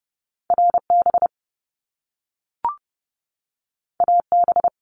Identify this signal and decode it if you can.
Morse code, heard as a keyed tone, about 30 wpm, 700 Hz. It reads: RB AB
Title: Top Two Letter Combinations in Words 30wpm